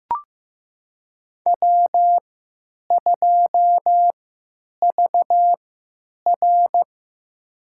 Morse code, heard as a keyed tone, about 15 wpm, spaced as for 9 wpm, 700 Hz. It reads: W2VR
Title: USA Call Signs - 4 Characters - 3x Character Spacing - 15wpm